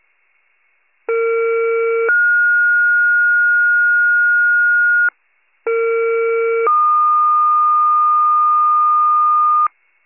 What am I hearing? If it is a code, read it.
Two-tone page, heard as a one-second tone, then a three-second tone, about 460 then 1430 Hz, about 460 then 1220 Hz, with noise. It is heard from 1.1 to 9.7 s.